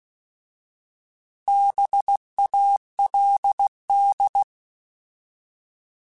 Morse code, heard as a keyed tone, about 16 wpm, 800 Hz. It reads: BALD
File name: Alphabetical / Alphabetical mp3s / Bs @15